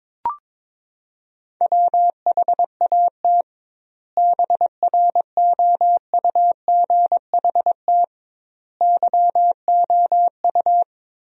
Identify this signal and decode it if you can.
Morse code, heard as a keyed tone, about 22 wpm, 700 Hz. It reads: WHAT BROUGHT YOU
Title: 2x Word-Spacing - Sentences from Top 500 Words - 22wpm